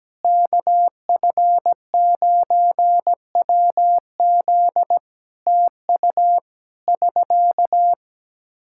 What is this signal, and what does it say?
Morse code, heard as a keyed tone, about 17 wpm, 700 Hz. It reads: KF9WZ TU <SK>
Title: K1USN SST QSO Practice 17wpm